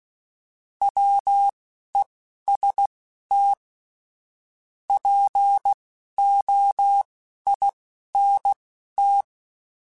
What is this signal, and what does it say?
Morse code, heard as a keyed tone, about 16 wpm, spaced as for 11 wpm, 800 Hz. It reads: WEST POINT